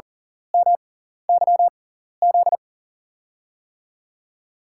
Morse code, heard as a keyed tone, about 40 wpm, spaced as for 14 wpm, 700 Hz. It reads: MYZ
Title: USA Call Signs - 6 Characters - 6x Character Spacing - 40wpm